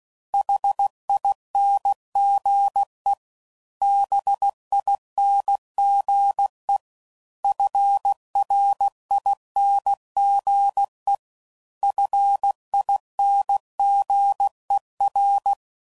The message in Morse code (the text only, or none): HINGE BINGE FRINGE FINGER